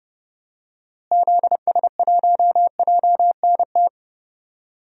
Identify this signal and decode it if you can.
Morse code, heard as a keyed tone, about 30 wpm, 700 Hz. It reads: ZS1JNT
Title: DXCC-100 Call Signs - 6 Characters - Repeated 3x - 30wpm